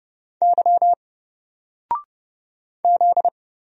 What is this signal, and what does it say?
Morse code, heard as a keyed tone, about 30 wpm, 700 Hz. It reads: Y Z
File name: Single-Letter-Number-30wpm